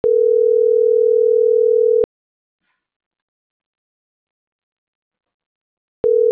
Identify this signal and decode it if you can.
Ringback tone (North America), 440 + 480 Hz.